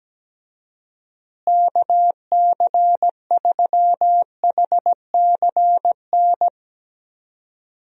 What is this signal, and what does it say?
Morse code, heard as a keyed tone, about 17 wpm, 700 Hz. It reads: KC3HCN
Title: USA Call Signs - 6 Characters - Repeated 2x - 17wpm